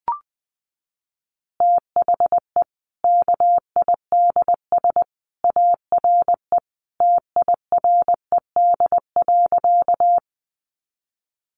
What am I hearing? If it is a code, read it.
Morse code, heard as a keyed tone, about 20 wpm, 700 Hz. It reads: THE KIDS ARE TIRED.
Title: Easy Phrases 20wpm